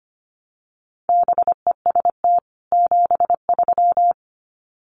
Morse code, heard as a keyed tone, about 25 wpm, 700 Hz. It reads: BEST 73
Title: Common QSO Phrases 25wpm